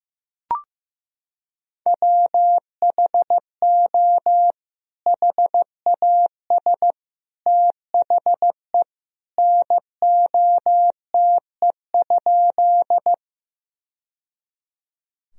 Morse code, heard as a keyed tone, about 15 wpm, 700 Hz. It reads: WHO HAS THE NOTE?
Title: Sentences from Top 1000 Words - Review Part 2 - 15wpm